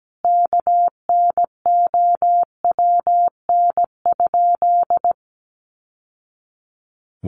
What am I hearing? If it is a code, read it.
Morse code, heard as a keyed tone, about 17 wpm, 700 Hz. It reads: KNOWN?
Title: Sentences from Top 1200 Words 17wpm